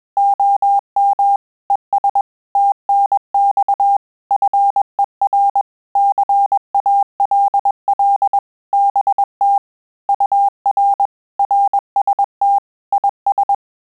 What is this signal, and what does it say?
Morse code, heard as a keyed tone, about 21 wpm, 800 Hz. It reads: OM ES TNX FER CALL BT UR RST IS
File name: QSO CW 100 CPM